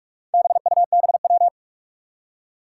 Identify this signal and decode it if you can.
Morse code, heard as a keyed tone, about 44 wpm, 700 Hz. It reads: BUBW
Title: Four Challenging Letter-Number Combinations 45wpm